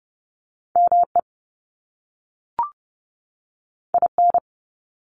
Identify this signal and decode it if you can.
Morse code, heard as a keyed tone, about 30 wpm, 700 Hz. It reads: ME IN
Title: US States 30wpm